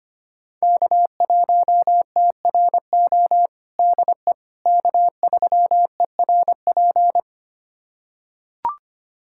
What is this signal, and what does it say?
Morse code, heard as a keyed tone, about 25 wpm, 700 Hz. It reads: K1TRO DE K3ERP
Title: USA Callsign DE Callsign 25wpm